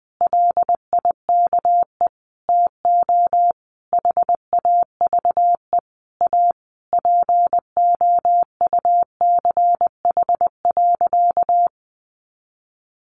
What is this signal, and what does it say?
Morse code, heard as a keyed tone, about 20 wpm, 700 Hz. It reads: LIKE TO HAVE A POUCH.
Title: Easy Phrases 20wpm